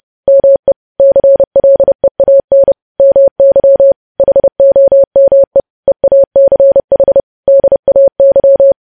Morse code, heard as a keyed tone, about 30 wpm, 550 Hz. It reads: ME CLEAN MY HOME EACH DAY
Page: news headlines in morse code at 30 wpm